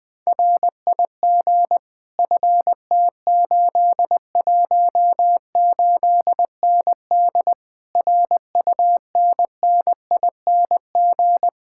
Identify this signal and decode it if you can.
Morse code, heard as a keyed tone, about 20 wpm, 700 Hz. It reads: RIG FT818ND RUNNING
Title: Speed-Racing Common QSO Phrases Encore - 20wpm